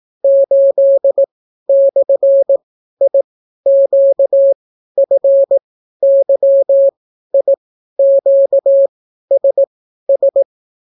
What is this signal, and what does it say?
Morse code, heard as a keyed tone, about 18 wpm, spaced as for 12 wpm, 550 Hz. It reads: 8/IQFYIQSS